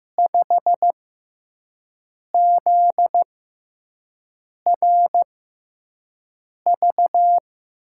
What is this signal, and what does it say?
Morse code, heard as a keyed tone, about 15 wpm, spaced as for 5 wpm, 700 Hz. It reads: 5ZRV
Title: USA Call Signs - 6 Characters - 6x Character Spacing - 15wpm